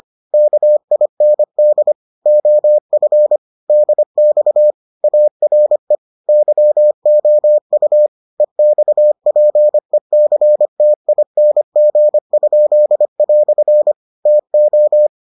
Morse code, heard as a keyed tone, about 25 wpm, 600 Hz. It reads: KIND OF DX ARE YOU EXPECTING?" TO